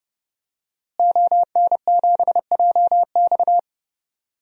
Morse code, heard as a keyed tone, about 30 wpm, 700 Hz. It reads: ON7JX
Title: DXCC-100 Call Signs - 5 Characters - Repeated 3x - 30wpm